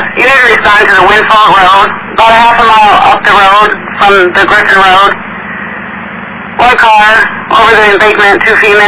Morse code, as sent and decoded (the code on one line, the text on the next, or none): none